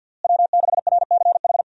Morse code, heard as a keyed tone, about 52 wpm, 700 Hz. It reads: W6RCH
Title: USA Call Signs - 5 Characters - Repeated 2x - 50wpm